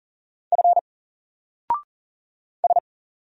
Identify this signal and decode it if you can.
Morse code, heard as a keyed tone, about 39 wpm, 700 Hz. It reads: F S
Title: Single Letter-Number 40wpm